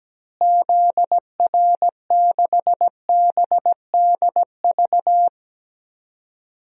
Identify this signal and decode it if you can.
Morse code, heard as a keyed tone, about 17 wpm, 700 Hz. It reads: ZR6BDV
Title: DXCC-100 Call Signs - 6 Characters - Repeated 2x - 17wpm